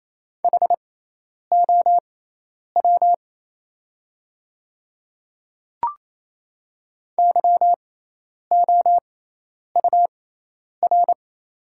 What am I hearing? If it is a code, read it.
Morse code, heard as a keyed tone, about 27 wpm, spaced as for 10 wpm, 700 Hz. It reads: HOW YOUR
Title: 6x Character Spacing - Top 100 Words - 28wpm